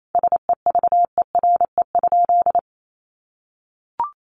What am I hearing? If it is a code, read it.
Morse code, heard as a keyed tone, about 28 wpm, 700 Hz. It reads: SEVERE?
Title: Sentences from Top 1400 Words 28wpm